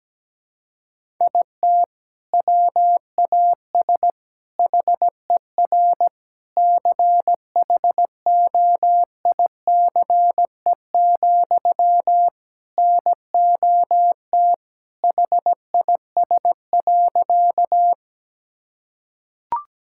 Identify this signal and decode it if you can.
Morse code, heard as a keyed tone, about 17 wpm, 700 Hz. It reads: IT WAS HER CHOICE, NOT HIS.
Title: Sentences from Top 1000 Words - Review Part 6 - 17wpm